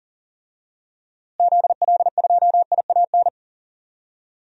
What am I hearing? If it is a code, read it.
Morse code, heard as a keyed tone, about 38 wpm, 700 Hz. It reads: ZL2IAN
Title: DXCC-100 Call Signs - 6 Characters - Repeated 3x - 40wpm